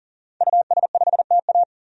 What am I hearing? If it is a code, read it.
Morse code, heard as a keyed tone, about 40 wpm, 700 Hz. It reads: US5TA
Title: DXCC-100 Call Signs - 5 Characters - 40wpm